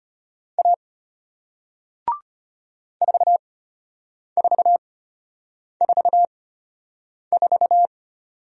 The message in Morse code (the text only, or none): A 4444